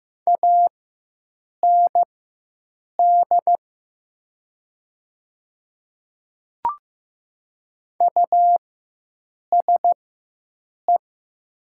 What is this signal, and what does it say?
Morse code, heard as a keyed tone, about 15 wpm, spaced as for 7 wpm, 700 Hz. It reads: AND USE
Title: 4x Character Spacing - Top 100 Words - 15wpm